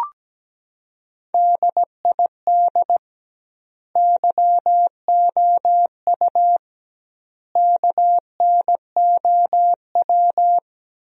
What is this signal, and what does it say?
Morse code, heard as a keyed tone, about 17 wpm, 700 Hz. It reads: DID YOU KNOW
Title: 2x Word-Spacing - Sentences from Top 100 Words - 17wpm